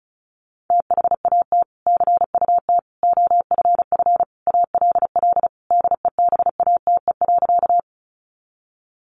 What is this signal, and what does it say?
Morse code, heard as a keyed tone, about 34 wpm, 700 Hz. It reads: THAT CUT OFF ALL DEBATE.
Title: Sentences from Top 1100 Words 35wpm